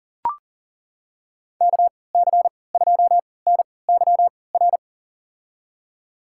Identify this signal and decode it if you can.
Morse code, heard as a keyed tone, about 40 wpm, spaced as for 23 wpm, 700 Hz. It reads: KC2NYR
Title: USA Call Signs - 6 Characters - 3x Character Spacing - 40wpm